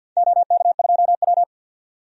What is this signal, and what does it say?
Morse code, heard as a keyed tone, about 50 wpm, 700 Hz. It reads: OK2W
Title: DXCC-100 Call Signs - 4 Characters - 50wpm